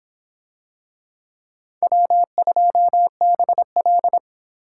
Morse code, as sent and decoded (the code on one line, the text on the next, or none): .-- ..--- -... .-..
W2BL